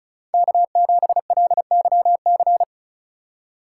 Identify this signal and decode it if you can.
Morse code, heard as a keyed tone, about 46 wpm, 700 Hz. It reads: K7LYC